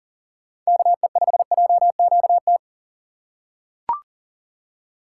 Morse code, heard as a keyed tone, about 40 wpm, 700 Hz. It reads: KE5JQT
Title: USA Call Signs - 6 Characters - 4x Character Spacing - 40wpm